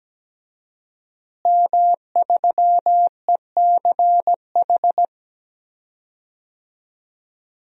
Morse code, heard as a keyed tone, about 17 wpm, 700 Hz. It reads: M3ECH
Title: DXCC-100 Call Signs - 5 Characters - Repeated 2x - 17wpm